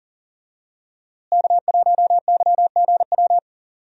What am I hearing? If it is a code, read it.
Morse code, heard as a keyed tone, about 40 wpm, 700 Hz. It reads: K1YGW
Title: USA Call Signs - 5 Characters - Repeated 3x - 40wpm